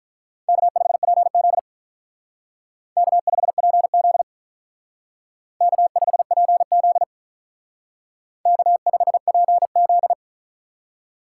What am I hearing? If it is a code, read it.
Morse code, heard as a keyed tone, about 43 wpm, 700 Hz. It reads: K5PZ K5PZ K5PZ K5PZ
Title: Speed Racing - USA 1x2 Call Signs 35wpm